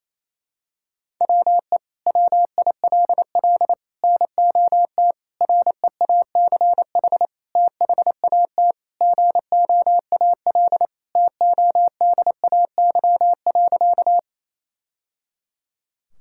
Morse code, heard as a keyed tone, about 28 wpm, 700 Hz. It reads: WE WILL NOT REACH THAT GOAL TODAY.
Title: Sentences from Top 1000 Words - Review Part 5 - 28wpm